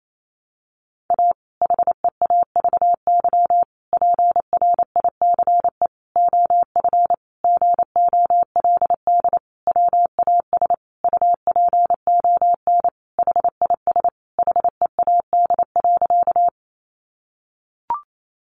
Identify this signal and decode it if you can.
Morse code, heard as a keyed tone, about 28 wpm, 700 Hz. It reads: A HEAVY PRICE OF GOLD WAS UPON HIS HEAD.
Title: Sentences from Top 1100 Words 28wpm